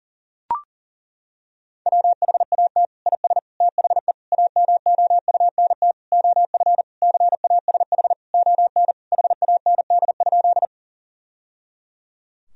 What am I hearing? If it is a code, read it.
Morse code, heard as a keyed tone, about 40 wpm, 700 Hz. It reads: WHAT IS THE AMOUNT OF CASH ON HAND?
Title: Sentences from Top 1000 Words - Review Part 4 - 40wpm